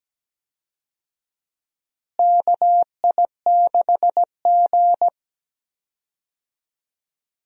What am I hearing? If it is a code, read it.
Morse code, heard as a keyed tone, about 17 wpm, 700 Hz. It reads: KI6G
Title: USA Call Signs - 4 Characters - Repeated 3x - 17wpm